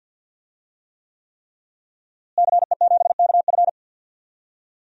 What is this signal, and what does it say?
Morse code, heard as a keyed tone, about 50 wpm, 700 Hz. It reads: CE7KF
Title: DXCC-100 Call Signs - 5 Characters - Repeated 3x - 50wpm